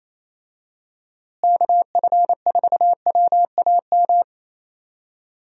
Morse code, heard as a keyed tone, about 28 wpm, 700 Hz. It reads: KF4WAM
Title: USA Call Signs - 6 Characters - Repeated 2x - 28wpm